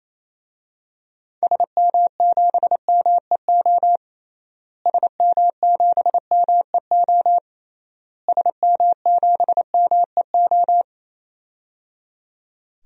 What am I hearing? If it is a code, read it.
Morse code, heard as a keyed tone, about 27 wpm, 700 Hz. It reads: SM7MEO SM7MEO SM7MEO
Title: DXCC-100 Call Signs - 6 Characters - Repeated 3x - 28wpm